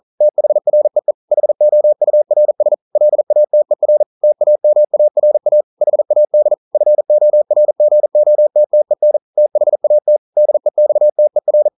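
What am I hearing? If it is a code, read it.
Morse code, heard as a keyed tone, about 41 wpm, 600 Hz. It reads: THREE HOURS LATER TAMARA HAD FORGOTTEN THAT DEXTER
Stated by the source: Morse code practice at 41 wpm, from CW med Gnister.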